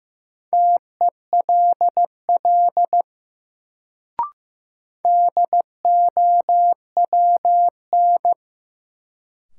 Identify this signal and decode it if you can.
Morse code, heard as a keyed tone, about 15 wpm, 700 Hz. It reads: TELL DOWN
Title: Sets of 2 Words - New Word Drill - 15wpm